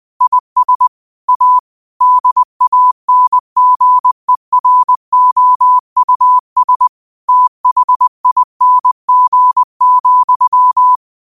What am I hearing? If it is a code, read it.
Morse code, heard as a keyed tone, about 20 wpm, 1000 Hz. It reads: IS A DANGEROUS THING,
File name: qotdfiles / 20wpm